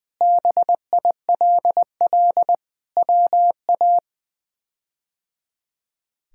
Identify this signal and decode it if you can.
Morse code, heard as a keyed tone, about 20 wpm, 700 Hz. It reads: BILL WA
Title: K1USN SST QSO Practice - 20wpm